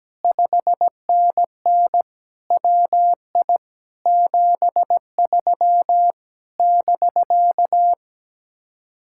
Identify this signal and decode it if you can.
Morse code, heard as a keyed tone, about 17 wpm, 700 Hz. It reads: 5NN WI 73 <BK>
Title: Speed-Racing Common QSO Phrases Encore - 17wpm